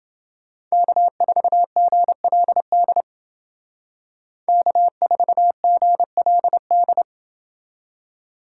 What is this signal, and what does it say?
Morse code, heard as a keyed tone, about 28 wpm, 700 Hz. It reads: K4GLD K4GLD